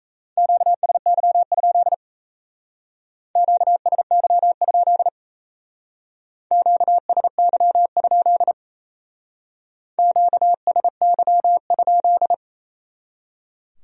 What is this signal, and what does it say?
Morse code, heard as a keyed tone, about 33 wpm, 700 Hz. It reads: QSY? QSY? QSY? QSY?
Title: Speed-Racing Q-Codes - 28wpm